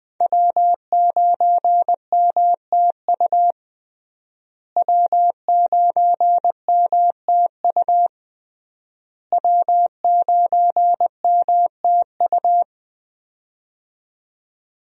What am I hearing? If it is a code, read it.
Morse code, heard as a keyed tone, about 20 wpm, 700 Hz. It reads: W9MTU W9MTU W9MTU
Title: USA Call Signs - 5 Characters - Repeated 3x - 20wpm